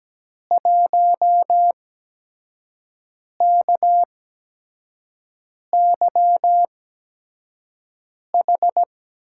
Morse code, heard as a keyed tone, about 17 wpm, spaced as for 5 wpm, 700 Hz. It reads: 1KYH
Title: USA Call Signs - 6 Characters - 8x Character Spacing - 17wpm